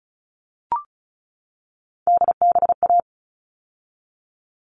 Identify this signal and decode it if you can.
Morse code, heard as a keyed tone, about 34 wpm, 700 Hz. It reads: DBA
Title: All Cut Numbers - 3-Digit 35wpm